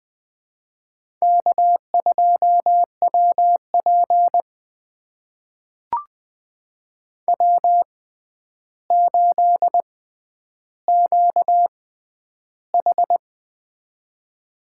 Morse code, heard as a keyed tone, about 20 wpm, spaced as for 10 wpm, 700 Hz. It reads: K2WP W8QH
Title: USA Call Signs - 4 Characters - 6x Character Spacing - 20wpm